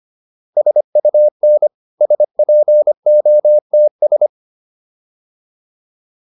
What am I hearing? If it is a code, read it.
Morse code, heard as a keyed tone, about 25 wpm, 600 Hz. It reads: SUN SPOTS